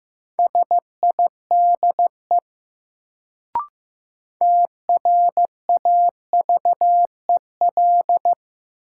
Morse code, heard as a keyed tone, about 15 wpm, 700 Hz. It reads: SIDE TRAVEL
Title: Sentences from Top 400 Words - New Word Drill - 15wpm